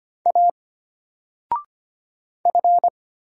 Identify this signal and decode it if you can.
Morse code, heard as a keyed tone, about 25 wpm, 700 Hz. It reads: A F
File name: Single-Letter-Number-25wpm